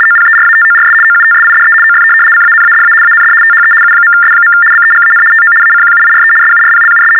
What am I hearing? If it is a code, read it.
RTTY, about 75 baud. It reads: OPERATORS OUTSIDE OF CANADA AND THE U.S. BEGAN TO ACQUIRE SURPLUS
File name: teletype-RTTY-75